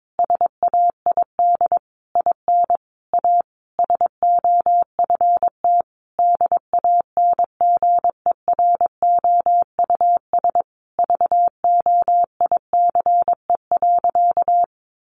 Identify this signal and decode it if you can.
Morse code, heard as a keyed tone, about 22 wpm, 700 Hz. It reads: SAID IN A SOFT DANGEROUS VOICE.
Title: Sentences from Top 1500 Words 22wpm